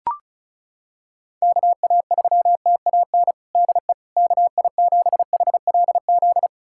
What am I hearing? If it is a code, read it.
Morse code, heard as a keyed tone, about 36 wpm, 700 Hz. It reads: KA3TAN DE KI7HLZ
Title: USA Callsign DE Callsign - 6 Characters - 35wpm